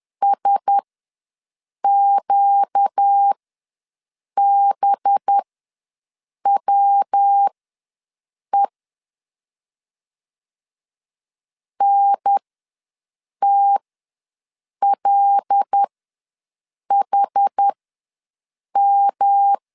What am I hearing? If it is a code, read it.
Morse code, heard as a keyed tone, about 10 wpm, spaced as for 6 wpm, 800 Hz. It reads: SQBWE NTLHM
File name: LEZIONI / 5 LBVC / 5 LB ws 10wpm 3616